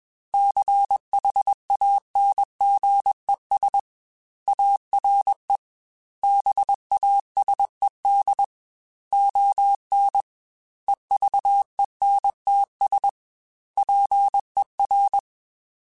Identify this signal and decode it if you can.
Morse code, heard as a keyed tone, about 21 wpm, 800 Hz. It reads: CHANGES ARE BASED ON EVENTS PER
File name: Technical Subjects / Technical Subjects-mp3s / Reactance @20